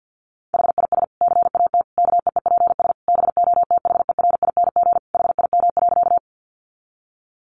Morse code, heard as a keyed tone, about 50 wpm, 700 Hz. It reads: HIS CAT KEEPS BOTHERING HIM.
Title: Additional Easy Phrases 50wpm